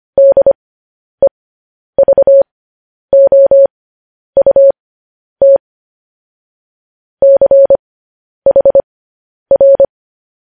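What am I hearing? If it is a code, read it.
Morse code, heard as a keyed tone, about 25 wpm, spaced as for 10 wpm, 550 Hz. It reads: DEVOUT CHR
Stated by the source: News headlines sent in Morse code at 10 wpm.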